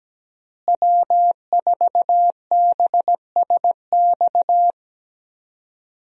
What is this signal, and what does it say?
Morse code, heard as a keyed tone, about 17 wpm, 700 Hz. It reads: W4BSX
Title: USA Call Signs - 5 Characters - Repeated 2x - 17wpm